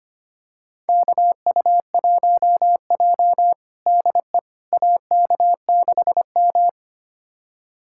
Morse code, heard as a keyed tone, about 25 wpm, 700 Hz. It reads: KU1J DE AK6M
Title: USA Callsign DE Callsign - 4 Characters - 25wpm